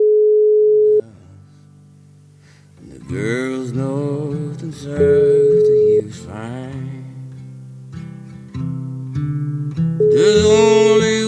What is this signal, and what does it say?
Ringback tone (Europe, 425 Hz).